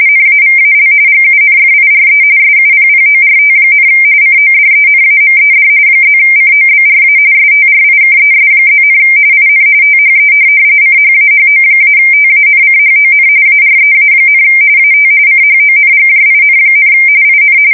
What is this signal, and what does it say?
RTTY, about 45.45 baud. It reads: NAVAL GUNFIRE SUPPORT, CODENAMED +COBRA STRIKES+, IN THE CHAHO-TANCHON AREA, AT CHONGJIN, IN
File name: fsk170 / USS-Missouri